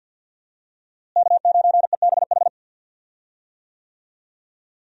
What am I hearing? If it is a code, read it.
Morse code, heard as a keyed tone, about 45 wpm, 700 Hz. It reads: K9EBH
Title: USA Call Signs - 5 Characters - Repeated 2x - 50wpm